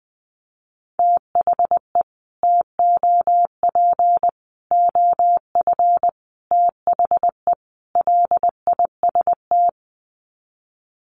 Morse code, heard as a keyed tone, about 20 wpm, 700 Hz. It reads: THE TOP OF THE LIST
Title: Sets of 5 Words Encore2 20wpm